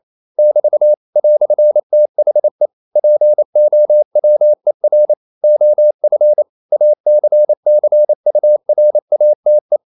Morse code, heard as a keyed tone, about 28 wpm, 600 Hz. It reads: <BT> "THE POWER OF ACCURATE
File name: CW / Practice / EnglishQuotations0128